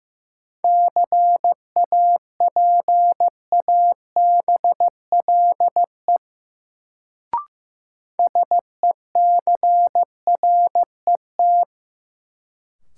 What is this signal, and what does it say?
Morse code, heard as a keyed tone, about 15 wpm, 700 Hz. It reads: CAPABLE SECRET
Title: Sentences from Top 1200 Words - New Word Drill - 15wpm